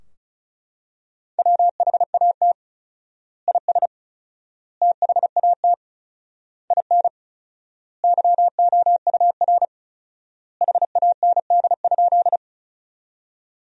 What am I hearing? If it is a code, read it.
Morse code, heard as a keyed tone, about 34 wpm, 700 Hz. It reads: WHAT IS THAT IN YOUR HAND?
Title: 4x Word-Spacing - Sentences from Top 200 Words - 35wpm